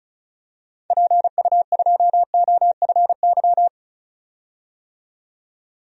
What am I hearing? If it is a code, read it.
Morse code, heard as a keyed tone, about 36 wpm, 700 Hz. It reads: PU2OFY
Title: DXCC-100 Call Signs - 6 Characters - 35wpm